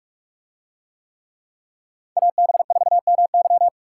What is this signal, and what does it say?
Morse code, heard as a keyed tone, about 42 wpm, 700 Hz. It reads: AB4MY